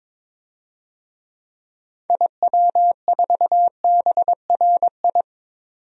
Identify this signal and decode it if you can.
Morse code, heard as a keyed tone, about 22 wpm, 700 Hz. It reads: IW4BRI
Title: DXCC-100 Call Signs - 6 Characters - 22wpm